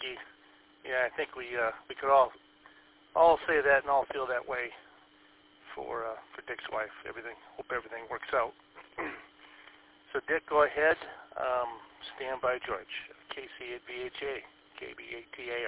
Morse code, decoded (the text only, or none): none